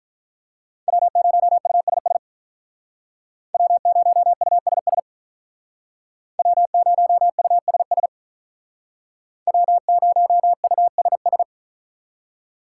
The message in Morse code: W0USS W0USS W0USS W0USS

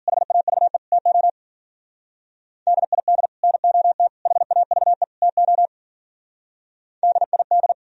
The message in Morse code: HAVE TO DID NOT HAVE TO DID

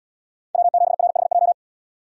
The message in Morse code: W7RHJ

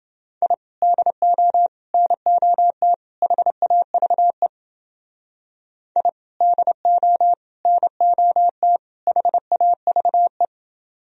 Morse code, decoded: I DO NOT HAVE I DO NOT HAVE

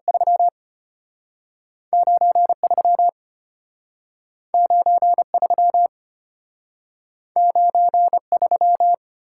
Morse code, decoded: 3 93 93 93

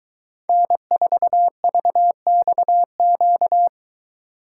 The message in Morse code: N4VXQ

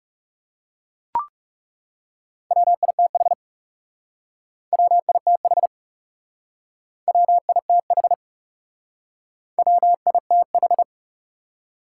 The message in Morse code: WITH WITH WITH WITH

